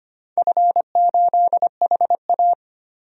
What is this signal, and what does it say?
Morse code, heard as a keyed tone, about 25 wpm, 700 Hz. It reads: F8HA
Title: DXCC-100 Call Signs - 4 Characters - 25wpm